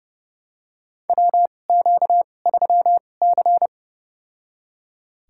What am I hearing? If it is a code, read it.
Morse code, heard as a keyed tone, about 30 wpm, spaced as for 22 wpm, 700 Hz. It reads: WQ3C
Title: USA Call Signs - 4 Characters - 2x Character Spacing - 30wpm